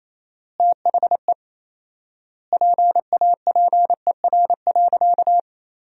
Morse code, heard as a keyed tone, about 28 wpm, 700 Hz. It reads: THE PAPER.
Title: 4x Word-Spacing - Sentences from Top 300 Words - 28wpm